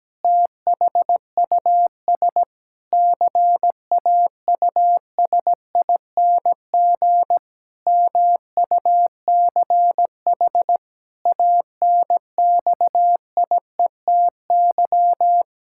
Morse code, heard as a keyed tone, about 17 wpm, 700 Hz. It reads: THUS CAUSING MUCH ANXIETY
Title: Pangram Challenge v2 - 17wpm